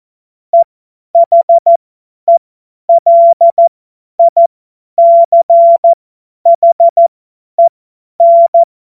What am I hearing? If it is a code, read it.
Morse code, heard as a keyed tone, about 14 wpm, spaced as for 10 wpm, 680 Hz. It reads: EHELICHEN
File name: Morsekurs / Witze / Witz 01. November 2022 50 F70